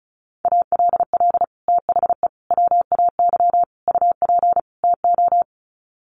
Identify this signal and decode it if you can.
Morse code, heard as a keyed tone, about 34 wpm, 700 Hz. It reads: ALL THE WAY UP TO
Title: Sets of 5 Words Encore 35wpm